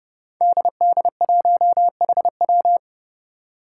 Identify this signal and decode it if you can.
Morse code, heard as a keyed tone, about 29 wpm, 700 Hz. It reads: DD1HW
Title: Five Challenging Letter-Number Combinations 30wpm